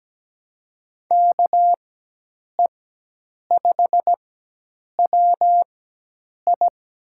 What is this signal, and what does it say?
Morse code, heard as a keyed tone, about 17 wpm, spaced as for 8 wpm, 700 Hz. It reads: KE5WI
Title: USA Call Signs - 5 Characters - 4x Character Spacing - 17wpm